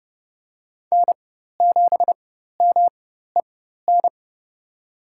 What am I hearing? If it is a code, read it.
Morse code, heard as a keyed tone, about 32 wpm, spaced as for 14 wpm, 700 Hz. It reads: N7MEN
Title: USA Call Signs - 4x Character Spacing - 30wpm